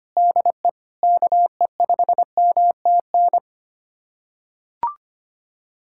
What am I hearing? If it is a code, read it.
Morse code, heard as a keyed tone, about 25 wpm, 700 Hz. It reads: DE KE5MTN
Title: USA Callsign DE Callsign - 6 Characters - 25wpm